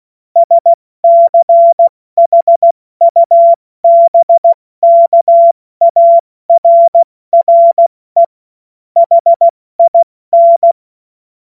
Morse code, heard as a keyed tone, about 16 wpm, 680 Hz. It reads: SCHUBKARRE HIN
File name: Morsekurs / Witze / Witz 27 Februar 2024 70 F80